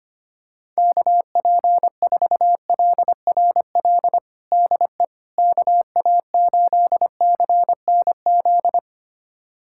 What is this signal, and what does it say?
Morse code, heard as a keyed tone, about 25 wpm, 700 Hz. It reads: KP4LRL DE KA8CNZ
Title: USA Callsign DE Callsign 25wpm